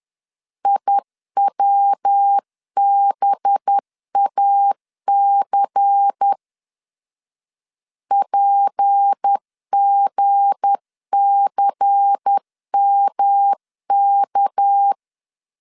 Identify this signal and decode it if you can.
Morse code, heard as a keyed tone, about 10 wpm, 800 Hz. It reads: IWBAC PGCMK
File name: LEZIONI / 7 GF9Z / 7 G 10wpm 2501